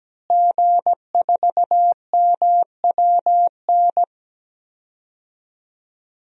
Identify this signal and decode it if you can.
Morse code, heard as a keyed tone, about 17 wpm, 700 Hz. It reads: G4MWN
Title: DXCC-100 Call Signs - 5 Characters - Repeated 3x - 17wpm